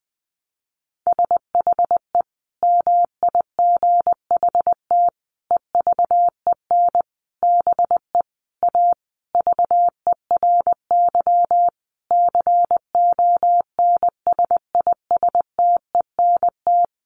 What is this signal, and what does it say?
Morse code, heard as a keyed tone, about 20 wpm, 700 Hz. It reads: SHE MIGHT EVEN BE A VERY CONSISTENT
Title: Sentences from Top 1500 Words 20wpm